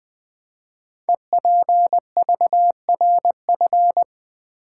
Morse code, heard as a keyed tone, about 20 wpm, 700 Hz. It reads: EPVRF
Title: Code Groups - 5 Letters - 20wpm